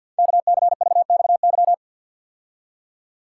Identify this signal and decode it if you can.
Morse code, heard as a keyed tone, about 49 wpm, 700 Hz. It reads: KCVXY
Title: Code Groups - 5 Letters - 50wpm